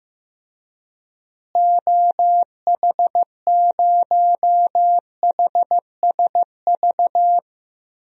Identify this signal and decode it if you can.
Morse code, heard as a keyed tone, about 15 wpm, 700 Hz. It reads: OH0HSV
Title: DXCC-100 Call Signs - 6 Characters - Repeated 2x - 15wpm